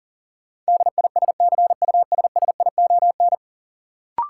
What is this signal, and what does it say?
Morse code, heard as a keyed tone, about 37 wpm, 700 Hz. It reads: DISCUSSION
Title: Sentences from Top 1100 Words - New Word Drill Alt - 40wpm